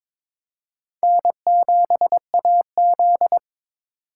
Morse code, heard as a keyed tone, about 21 wpm, 700 Hz. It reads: N7AZ
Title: K1USN SST QSO Practice - 22wpm